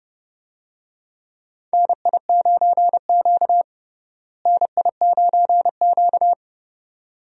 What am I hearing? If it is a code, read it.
Morse code, heard as a keyed tone, about 29 wpm, 700 Hz. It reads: NI9Q NI9Q
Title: USA Call Signs - 4 Characters - Repeated 2x - 30wpm